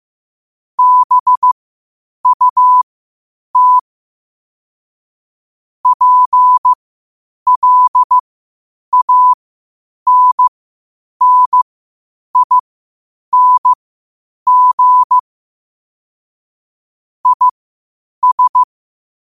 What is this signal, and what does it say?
Morse code, heard as a keyed tone, about 15 wpm, spaced as for 8 wpm, 1000 Hz. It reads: BUT PLANNING IS
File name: qotdfiles / 8wpm